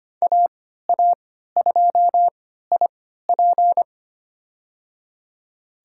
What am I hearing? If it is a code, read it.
Morse code, heard as a keyed tone, about 25 wpm, spaced as for 14 wpm, 700 Hz. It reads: AA2IP
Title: USA Call Signs - 5 Characters - 3x Character Spacing - 25wpm